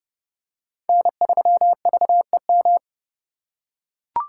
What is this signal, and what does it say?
Morse code, heard as a keyed tone, about 29 wpm, 700 Hz. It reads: N3VEM